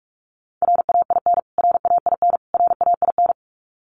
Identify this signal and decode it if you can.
Morse code, heard as a keyed tone, about 44 wpm, 700 Hz. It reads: RAIN RAIN RAIN
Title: Top 500 Words Repeated 45wpm